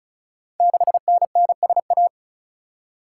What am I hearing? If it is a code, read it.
Morse code, heard as a keyed tone, about 36 wpm, 690 Hz. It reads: 6NNSA
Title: Code Groups - 5 Letter:Numbers - 35wpm